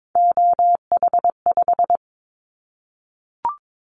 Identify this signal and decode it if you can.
Morse code, heard as a keyed tone, about 22 wpm, 700 Hz. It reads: OH5
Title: Callsign Prefixes for Top 25 DXCC - Repeated 3x - 22wpm